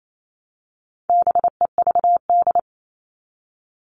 Morse code, heard as a keyed tone, about 28 wpm, 700 Hz. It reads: BEVD
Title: All Cut Numbers - 4-Digit 28wpm